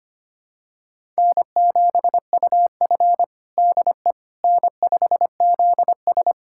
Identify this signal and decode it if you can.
Morse code, heard as a keyed tone, about 25 wpm, 700 Hz. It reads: N7UF DE N5ZS
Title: USA Callsign DE Callsign - 4 Characters - 25wpm